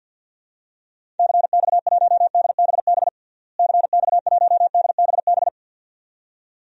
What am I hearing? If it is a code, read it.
Morse code, heard as a keyed tone, about 49 wpm, 700 Hz. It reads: CX1DBB CX1DBB
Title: DXCC-100 Call Signs - 6 Characters - Repeated 2x - 50wpm